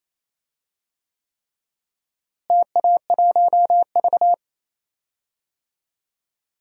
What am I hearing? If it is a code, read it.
Morse code, heard as a keyed tone, about 28 wpm, 700 Hz. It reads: TA1V
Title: DXCC-100 Call Signs - 4 Characters - Repeated 3x - 28wpm